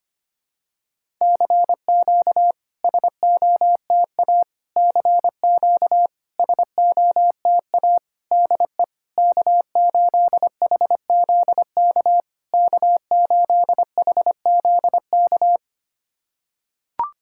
Morse code, heard as a keyed tone, about 25 wpm, 700 Hz. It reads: CQ SOTA CQ SOTA DE K8HZK K8HZK
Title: SOTA QSO Practice 25wpm